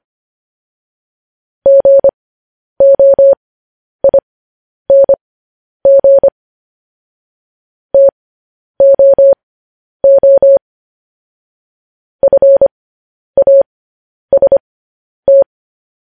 Morse code, heard as a keyed tone, about 25 wpm, spaced as for 10 wpm, 550 Hz. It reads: GOING TOO FAST
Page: news headlines in morse code at 10 wpm